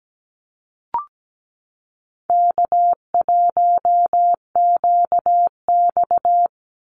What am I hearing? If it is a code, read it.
Morse code, heard as a keyed tone, about 17 wpm, 700 Hz. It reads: K1QX
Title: Easy Call Signs Repeated 3x - 17wpm